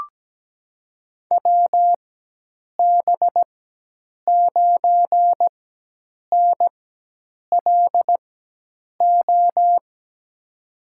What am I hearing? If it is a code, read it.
Morse code, heard as a keyed tone, about 17 wpm, spaced as for 8 wpm, 700 Hz. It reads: WB9NLO
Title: USA Call Signs - 4x Character Spacing - 17wpm